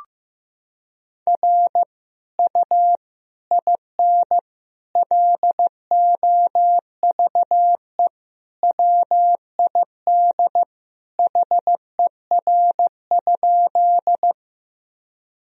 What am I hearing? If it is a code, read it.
Morse code, heard as a keyed tone, about 16 wpm, 700 Hz. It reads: R U IN LOVE WID HER?
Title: Sentences from Top 400 Words and Abbreviations 15wpm